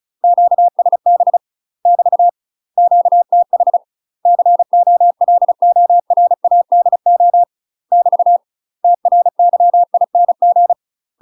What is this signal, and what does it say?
Morse code, heard as a keyed tone, about 35 wpm, 700 Hz. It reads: QSB <BT> QTH COLORADO <BT> TRYING
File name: UK MASTER QSOs TRAINING / 35WPM / UK MASTER QSOs 0007